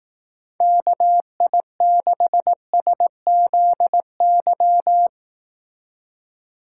Morse code, heard as a keyed tone, about 18 wpm, 700 Hz. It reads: KI6SZY
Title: Speed Racing - USA 2x3 Call Signs 15wpm